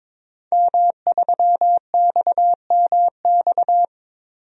Morse code, heard as a keyed tone, about 22 wpm, 700 Hz. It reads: M3XMX
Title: DXCC-100 Call Signs - 5 Characters - 22wpm